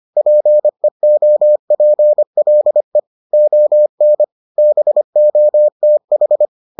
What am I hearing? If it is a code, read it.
Morse code, heard as a keyed tone, about 25 wpm, 600 Hz. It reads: PEOPLE ON BOTH